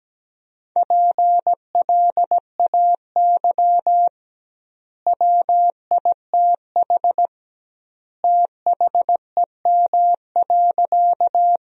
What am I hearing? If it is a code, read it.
Morse code, heard as a keyed tone, about 17 wpm, 700 Hz. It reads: PLAY WITH THEM.
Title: 2x Word-Spacing - Sentences from Top 200 Words - 17wpm